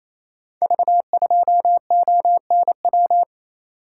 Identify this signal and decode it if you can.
Morse code, heard as a keyed tone, about 27 wpm, 700 Hz. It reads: V2ONW
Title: Code Groups - 5 Letter:Numbers - 28wpm